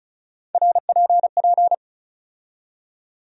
Morse code, heard as a keyed tone, about 34 wpm, 700 Hz. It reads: RPP
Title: USA Call Sign Postfixes - 3 Letters 35wpm